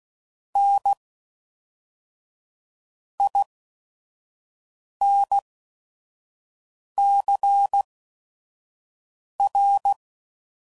Morse code, heard as a keyed tone, about 16 wpm, spaced as for 5 wpm, 800 Hz. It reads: N INCR
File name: CW Code Groups @5WPM / Technical Subjects / XC